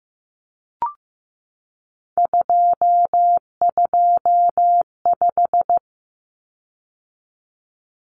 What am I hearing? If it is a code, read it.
Morse code, heard as a keyed tone, about 15 wpm, 700 Hz. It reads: 225
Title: Common Cut Numbers - 3-Digit 15wpm